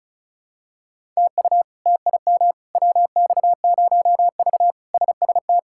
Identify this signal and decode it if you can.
Morse code, heard as a keyed tone, about 34 wpm, 700 Hz. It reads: TU TIM WX0V SST
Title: K1USN SST QSO Practice - 35wpm